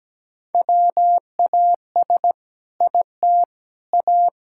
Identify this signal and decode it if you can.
Morse code, heard as a keyed tone, about 17 wpm, 700 Hz. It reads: WAS IT A
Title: Sentences from Top 1000 Words - Review Part 5 - 17wpm